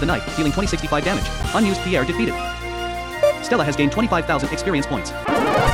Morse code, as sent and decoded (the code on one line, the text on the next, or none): none